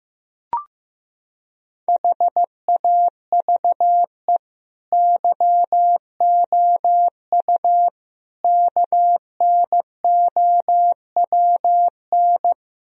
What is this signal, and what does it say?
Morse code, heard as a keyed tone, about 16 wpm, 700 Hz. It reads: HAVE YOU KNOWN
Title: Sentences from Top 1000 Words - Review Part 3 - 15wpm